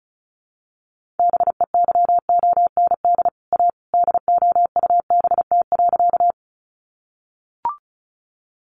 Morse code, heard as a keyed tone, about 34 wpm, 700 Hz. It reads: BEYOND A DOUBT.
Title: Sentences from Top 1500 Words 35wpm